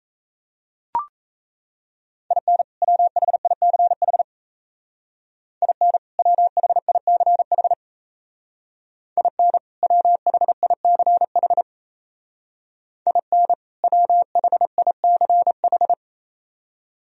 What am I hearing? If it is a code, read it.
Morse code, heard as a keyed tone, about 35 wpm, 700 Hz. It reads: IN WHICH IN WHICH IN WHICH IN WHICH
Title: Speed-Racing - Sets of 2 Words 28wpm